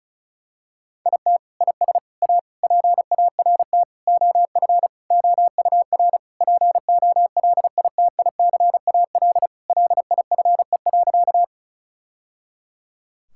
Morse code, heard as a keyed tone, about 36 wpm, 700 Hz. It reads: IT IS A PART OF OUR POLITICAL LIFE.
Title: Sentences from Top 1000 Words - Review Part 5 - 35wpm